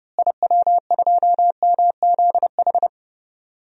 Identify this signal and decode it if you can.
Morse code, heard as a keyed tone, about 30 wpm, 700 Hz. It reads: IW2MZH
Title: DXCC-100 Call Signs - 6 Characters - Repeated 3x - 30wpm